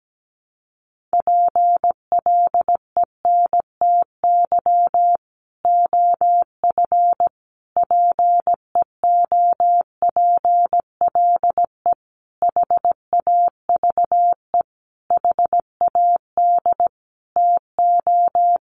Morse code, heard as a keyed tone, about 17 wpm, 700 Hz. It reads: PLENTY OF PEOPLE HAVE HAD TO
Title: Sentences from Top 1300 Words 17wpm